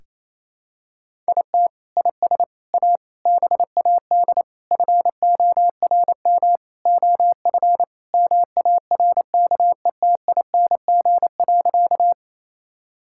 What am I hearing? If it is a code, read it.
Morse code, heard as a keyed tone, about 28 wpm, 700 Hz. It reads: IT IS A BAD FORM OF MARKETING.
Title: Sentences from Top 1000 Words - Review Part 5 - 28wpm